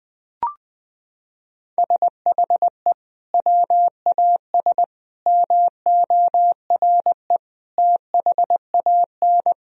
Morse code, heard as a keyed tone, about 20 wpm, 700 Hz. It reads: SHE WAS MORE THAN
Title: Sentences from Top 1000 Words - Review Part 3 - 20wpm